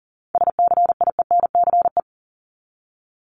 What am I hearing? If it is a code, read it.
Morse code, heard as a keyed tone, about 40 wpm, 700 Hz. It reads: SCIENCE
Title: Top 500 Words 40wpm